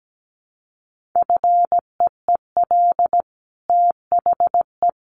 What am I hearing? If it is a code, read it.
Morse code, heard as a keyed tone, about 17 wpm, 700 Hz. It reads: FEEL THE
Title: Sentences from Top 1400 Words 17wpm